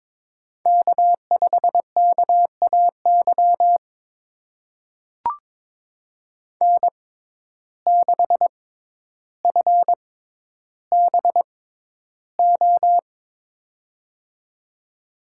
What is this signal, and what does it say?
Morse code, heard as a keyed tone, about 22 wpm, spaced as for 11 wpm, 700 Hz. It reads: K5KAY N6FBO
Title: USA Call Signs - 5 Characters - 6x Character Spacing - 22wpm